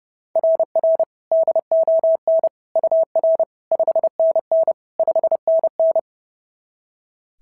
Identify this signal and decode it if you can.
Morse code, heard as a keyed tone, about 26 wpm, 650 Hz. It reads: RR DON UR 5NN 5NN
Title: SOTA QSO Practice 30wpm